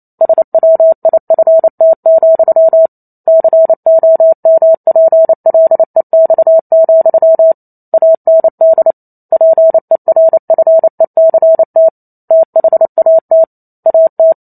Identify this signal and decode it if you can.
Morse code, heard as a keyed tone, about 28 wpm, 650 Hz. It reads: SWIFT, COMPLEX, AND PERFECT THAT AT